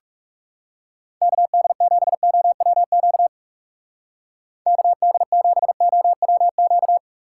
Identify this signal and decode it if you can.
Morse code, heard as a keyed tone, about 36 wpm, 700 Hz. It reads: KD7OWQ KD7OWQ
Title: Speed Racing - USA 2x3 Call Signs 30wpm